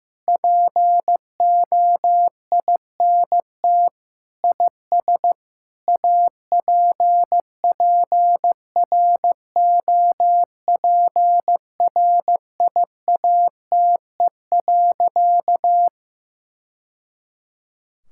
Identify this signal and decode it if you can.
Morse code, heard as a keyed tone, about 15 wpm, 700 Hz. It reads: POINT IS APPROPRIATE.
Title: Sentences from Top 1000 Words - Review Part 5 - 15wpm